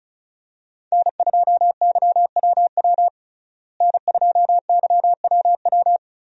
Morse code, heard as a keyed tone, about 36 wpm, 700 Hz. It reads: N2YWW N2YWW
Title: USA Call Signs - 5 Characters - Repeated 2x - 35wpm